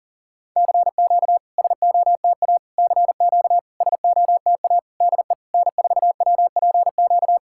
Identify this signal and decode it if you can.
Morse code, heard as a keyed tone, about 43 wpm, 700 Hz. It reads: CQ SOTA CQ SOTA DE N4WPQ
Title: SOTA QSO Practice 40wpm